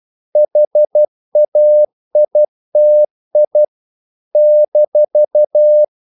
Morse code, heard as a keyed tone, about 12 wpm, 600 Hz. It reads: HAITI -
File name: CW / Practice / NationalCapitals12